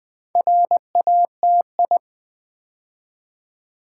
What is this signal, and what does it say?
Morse code, heard as a keyed tone, about 20 wpm, 690 Hz. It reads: RATI
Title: Build A Word - 2 Letters At A Time - 8-Letter Words - 20wpm